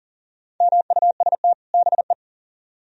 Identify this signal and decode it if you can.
Morse code, heard as a keyed tone, about 52 wpm, 700 Hz. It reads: MUST BE